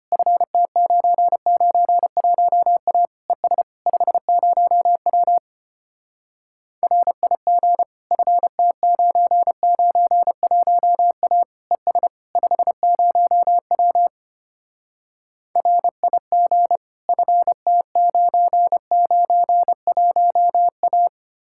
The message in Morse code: FT991A ES 50W RIG FT991A ES 50W RIG FT991A

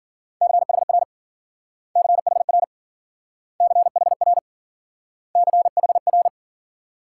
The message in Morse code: CHR CHR CHR CHR

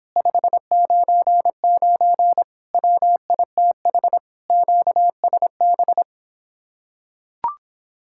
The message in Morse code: 599 WITH QSB

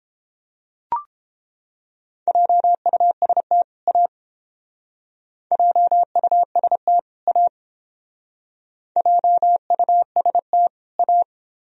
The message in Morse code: JUST A JUST A JUST A

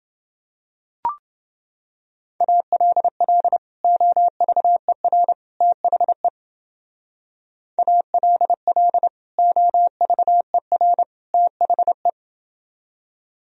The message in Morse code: ALL OVER THE ALL OVER THE

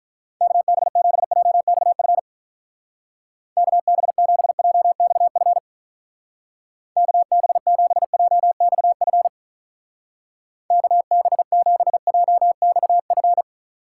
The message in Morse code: KB7JXF KB7JXF KB7JXF KB7JXF

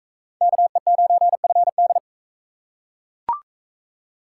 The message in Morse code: KE9FD

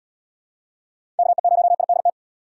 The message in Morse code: KE0IDT

